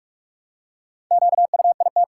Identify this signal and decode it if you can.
Morse code, heard as a keyed tone, about 46 wpm, 700 Hz. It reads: QUIT
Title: Four Letter Words - 45wpm